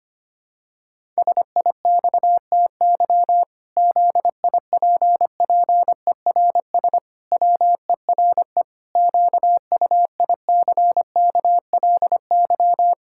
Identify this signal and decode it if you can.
Morse code, heard as a keyed tone, about 25 wpm, 700 Hz. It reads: SIXTY ZIPPERS WERE QUICKLY
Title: Pangram Challenge v2 - 25wpm